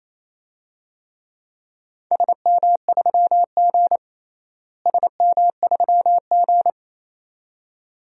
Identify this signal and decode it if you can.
Morse code, heard as a keyed tone, about 28 wpm, 700 Hz. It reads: SM3G SM3G
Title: DXCC-100 Call Signs - 4 Characters - Repeated 2x - 28wpm